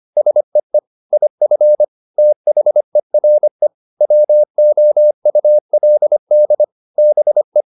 Morse code, heard as a keyed tone, about 25 wpm, 600 Hz. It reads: SEE IF THERE WOULD BE